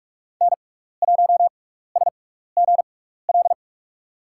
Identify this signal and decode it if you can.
Morse code, heard as a keyed tone, about 46 wpm, spaced as for 15 wpm, 700 Hz. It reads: N1SGL
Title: USA Call Signs - 5 Characters - 6x Character Spacing - 45wpm